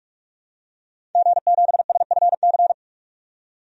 Morse code, heard as a keyed tone, about 46 wpm, 700 Hz. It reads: G7SFC
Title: DXCC-100 Call Signs - 5 Characters - 45wpm